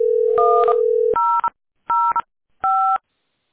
DTMF, heard as telephone touch tones, tones of about 340 ms. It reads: 1005